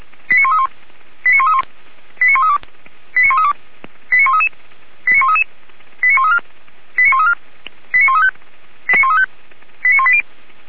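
Five-tone selective call (ZVEI), heard as quick runs of five tones, tones of about 70 ms. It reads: 89131 89131 89132 89132 89133 89133 89134 89134 89135 89135 89180
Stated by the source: ZVEI five-tone selective call received on an SDR.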